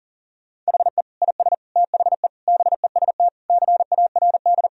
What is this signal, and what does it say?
Morse code, heard as a keyed tone, about 40 wpm, 700 Hz. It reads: HE IS THE BEST CARD
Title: Sentences from Top 1000 Words - Review Part 4 - 40wpm